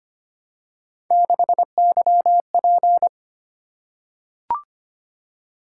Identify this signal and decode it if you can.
Morse code, heard as a keyed tone, about 25 wpm, 700 Hz. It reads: 6YP
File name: Three Letter-Number Combinations - 25wpm